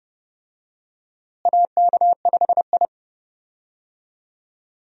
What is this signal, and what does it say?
Morse code, heard as a keyed tone, about 31 wpm, 700 Hz. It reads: AK5I
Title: USA Call Signs - 4 Characters - Repeated 3x - 30wpm